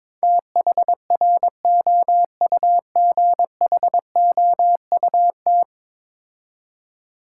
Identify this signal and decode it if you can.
Morse code, heard as a keyed tone, about 21 wpm, 700 Hz. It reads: THROUGHOUT
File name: practice / 22wpm / 10 Letter Words - First 3 Letters Given - 22wpm